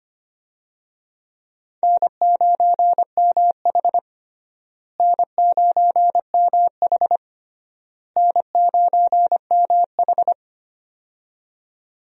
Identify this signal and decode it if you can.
Morse code, heard as a keyed tone, about 25 wpm, 700 Hz. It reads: N9MH N9MH N9MH
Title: USA Call Signs - 4 Characters - Repeated 3x - 25wpm